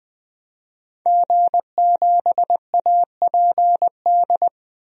Morse code, heard as a keyed tone, about 20 wpm, 700 Hz. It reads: G7APD
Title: DXCC-100 Call Signs - 5 Characters - Repeated 3x - 20wpm